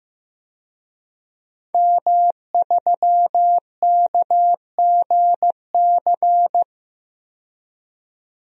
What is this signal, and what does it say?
Morse code, heard as a keyed tone, about 15 wpm, 700 Hz. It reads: M3KGC
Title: DXCC-100 Call Signs - 5 Characters - 15wpm